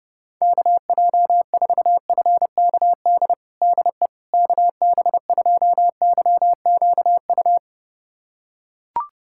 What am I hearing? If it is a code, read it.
Morse code, heard as a keyed tone, about 30 wpm, 700 Hz. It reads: KJ4FKD DE KB2YQU
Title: USA Callsign DE Callsign 30wpm